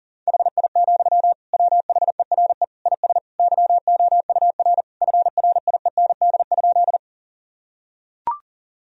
Morse code, heard as a keyed tone, about 40 wpm, 700 Hz. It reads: HI, WHERE IS YOUR FRIEND?
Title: Sentences from Top 1000 Words - Review Part 3 - 40wpm